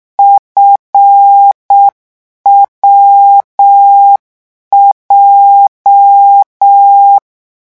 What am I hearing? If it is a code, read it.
Morse code, heard as a keyed tone, about 6 wpm, 800 Hz. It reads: FWJ